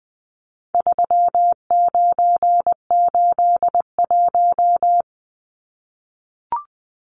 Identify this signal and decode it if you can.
Morse code, heard as a keyed tone, about 20 wpm, 700 Hz. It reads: 3981
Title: Numbers - 4-Digit 20wpm